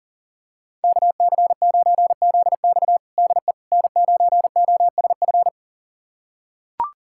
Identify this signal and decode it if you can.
Morse code, heard as a keyed tone, about 40 wpm, 690 Hz. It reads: KC9ZX DE N9OSF
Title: USA Callsign DE Callsign - 5 Characters - 40wpm